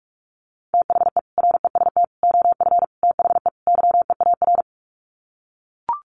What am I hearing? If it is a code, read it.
Morse code, heard as a keyed tone, about 44 wpm, 700 Hz. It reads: THE REST OF THE YEAR
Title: Sets of 5 Words Encore 45wpm